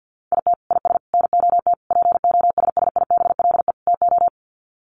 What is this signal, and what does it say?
Morse code, heard as a keyed tone, about 50 wpm, 700 Hz. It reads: IT IS NOT POSSIBLE TO
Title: Sets of 5 Words Encore2 50wpm